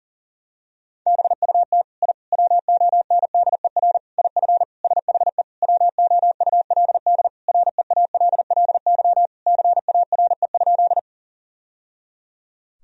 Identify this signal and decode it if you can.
Morse code, heard as a keyed tone, about 37 wpm, 700 Hz. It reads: BUT I WONDER IF SHE WOULD REALLY CARE?
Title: Sentences from Top 1000 Words - Review Part 3 - 40wpm